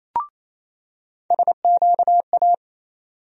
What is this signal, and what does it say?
Morse code, heard as a keyed tone, about 28 wpm, 700 Hz. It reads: SQA
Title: USA Call Sign Postfixes - 3 Letters 28wpm